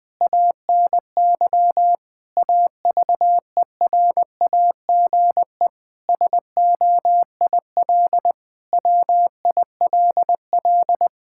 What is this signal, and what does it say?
Morse code, heard as a keyed tone, about 20 wpm, 700 Hz. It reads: ANY AVERAGE SOIL WILL
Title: Sentences from Top 1000 Words - Review Part 6 - 20wpm